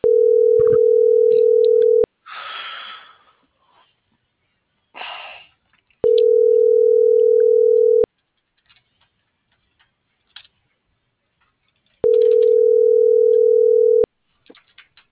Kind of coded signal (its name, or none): telephone call progress tone